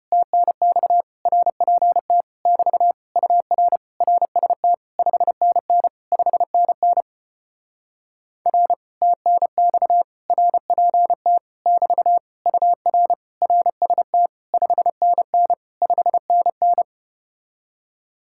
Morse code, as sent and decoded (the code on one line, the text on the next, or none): - -. -..- / .-. .--. - / -...- / ..- .-. / .-. ... - / ..... -. -. / ..... -. -. / .-. / - -. -..- / .-. .--. - / -...- / ..- .-. / .-. ... - / ..... -. -. / ..... -. -.
TNX RPT <BT> UR RST 5NN 5NN R TNX RPT <BT> UR RST 5NN 5NN